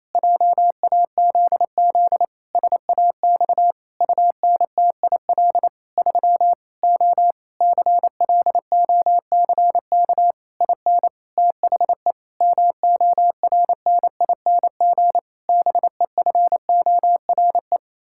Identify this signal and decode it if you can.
Morse code, heard as a keyed tone, about 28 wpm, 700 Hz. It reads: JAZZ SAX UNTIL 3 O CLOCK IN THE MORNING BEFORE
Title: Pangram Challenge v2 - 28wpm